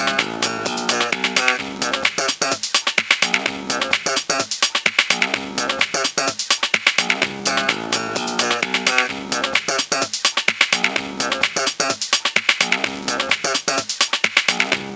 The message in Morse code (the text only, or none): none